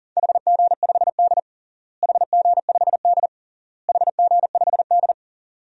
Morse code, heard as a keyed tone, about 40 wpm, 700 Hz. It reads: HG5D HG5D HG5D
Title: DXCC-100 Call Signs - 4 Characters - Repeated 3x - 40wpm